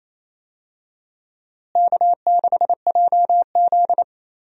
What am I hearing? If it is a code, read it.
Morse code, heard as a keyed tone, about 28 wpm, 700 Hz. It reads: K6JZ
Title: USA Call Signs - 4 Characters - Repeated 2x - 28wpm